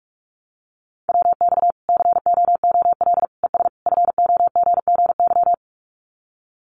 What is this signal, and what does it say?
Morse code, heard as a keyed tone, about 44 wpm, 700 Hz. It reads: WX COOL ES FOGGY
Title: Common QSO Phrases 45wpm